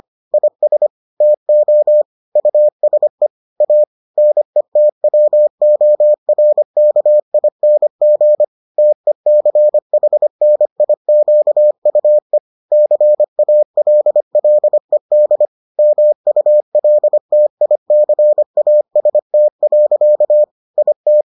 Morse code, heard as a keyed tone, about 25 wpm, 600 Hz. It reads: IS TO USE A NETWORKING TECHNIQUE CALLED MULTICAST. IT